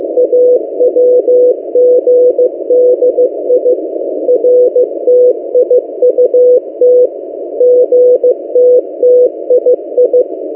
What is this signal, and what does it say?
Morse code, heard as a keyed tone, about 15 wpm, 500 Hz. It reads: AWGDI RTIUT GTTII
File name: M08a- -1355utc-8096khz